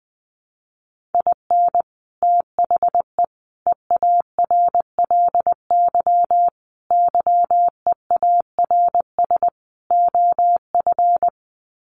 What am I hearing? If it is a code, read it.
Morse code, heard as a keyed tone, about 20 wpm, 700 Hz. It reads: IN THE EARLY YEARS OF
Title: Sets of 5 Words Encore 20wpm